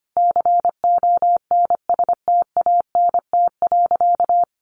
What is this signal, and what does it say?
Morse code, heard as a keyed tone, about 25 wpm, 700 Hz. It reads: CONSTANT.
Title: Sentences from Top 1500 Words 25wpm